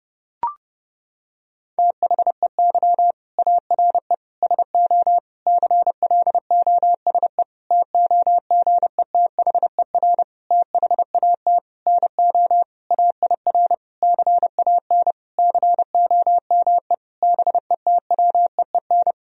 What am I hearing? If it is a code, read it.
Morse code, heard as a keyed tone, about 30 wpm, 700 Hz. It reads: THEY ARE SO CLOSE TOGETHER THAT NO AIR CAN COME BETWEEN
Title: Sentences from Top 1000 Words - Review Part 1 - 30wpm